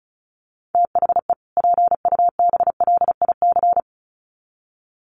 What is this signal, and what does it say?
Morse code, heard as a keyed tone, about 34 wpm, 700 Hz. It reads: THE PUBLIC
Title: Sets of 2 Words 35wpm